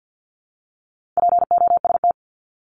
Morse code, heard as a keyed tone, about 50 wpm, 700 Hz. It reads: POST
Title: Top 751-1000 Words 50wpm